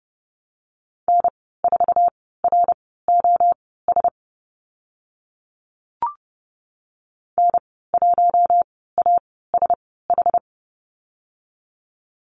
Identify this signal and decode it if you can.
Morse code, heard as a keyed tone, about 30 wpm, spaced as for 17 wpm, 700 Hz. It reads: N4ROS N1ASH
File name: Lesson 14g - Number 2 - Call Signs 30wpm